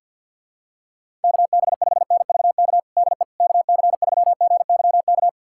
Morse code, heard as a keyed tone, about 49 wpm, 700 Hz. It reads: KB5NVK DE KC3GYK
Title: USA Callsign DE Callsign 50wpm